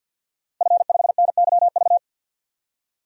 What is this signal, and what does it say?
Morse code, heard as a keyed tone, about 52 wpm, 700 Hz. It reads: F5NYV
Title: DXCC-100 Call Signs - 5 Characters - Repeated 3x - 50wpm